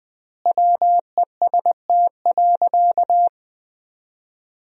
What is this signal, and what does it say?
Morse code, heard as a keyed tone, about 20 wpm, 700 Hz. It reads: WEST.
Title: 4x Word-Spacing - Sentences from Top 400 Words - 20wpm